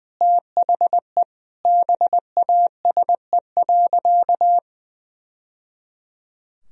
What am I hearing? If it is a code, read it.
Morse code, heard as a keyed tone, about 20 wpm, 700 Hz. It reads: THE BASE.
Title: Sentences from Top 1000 Words - Review Part 1 - 20wpm